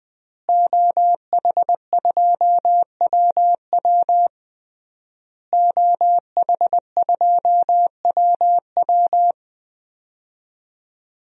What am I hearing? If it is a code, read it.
Morse code, heard as a keyed tone, about 20 wpm, 700 Hz. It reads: OH2WW OH2WW
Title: DXCC-100 Call Signs - 5 Characters - Repeated 3x - 20wpm